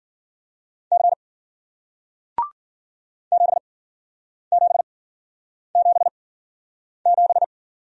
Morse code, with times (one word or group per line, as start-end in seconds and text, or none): 0.9-1.1 C
3.3-7.4 7777